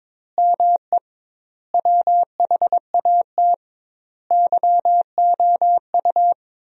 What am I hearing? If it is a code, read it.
Morse code, heard as a keyed tone, about 22 wpm, 700 Hz. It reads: ME WHAT YOU
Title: 2x Word-Spacing - Sentences from Top 200 Words - 22wpm